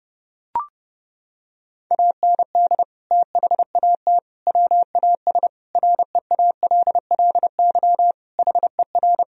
Morse code, heard as a keyed tone, about 30 wpm, 700 Hz. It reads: AND THAT WAS REALLY HER
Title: Sentences from Top 1000 Words - Review Part 3 - 30wpm